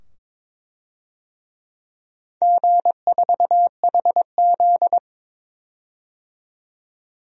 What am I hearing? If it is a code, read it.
Morse code, heard as a keyed tone, about 22 wpm, 700 Hz. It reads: G4HZ
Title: DXCC-100 Call Signs - 4 Characters - 22wpm